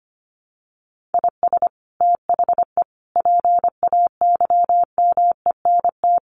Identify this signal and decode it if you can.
Morse code, heard as a keyed tone, about 25 wpm, 700 Hz. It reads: IS THE PAYMENT
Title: Sentences from Top 1500 Words 25wpm